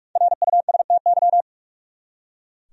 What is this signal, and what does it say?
Morse code, heard as a keyed tone, about 46 wpm, 700 Hz. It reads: RUSTY
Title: USA Common Names - 45wpm